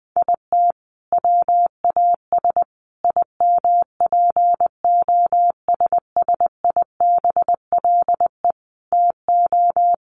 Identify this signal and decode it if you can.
Morse code, heard as a keyed tone, about 20 wpm, 700 Hz. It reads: IT WAS IMPOSSIBLE TO
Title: Sentences from Top 1400 Words 20wpm